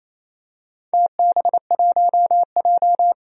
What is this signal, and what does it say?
Morse code, heard as a keyed tone, about 28 wpm, 700 Hz. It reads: TB1J
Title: DXCC-100 Call Signs - 4 Characters - 28wpm